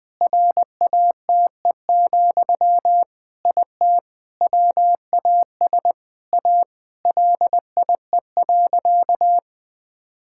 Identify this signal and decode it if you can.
Morse code, heard as a keyed tone, about 20 wpm, 700 Hz. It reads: RATE, IT WAS A LIE.
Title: Sentences from Top 1600 Words 20wpm